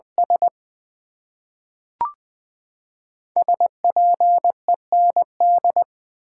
Morse code, heard as a keyed tone, about 20 wpm, 700 Hz. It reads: S SPEND
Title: Build A Phrase - 5-Word Phrases - 20wpm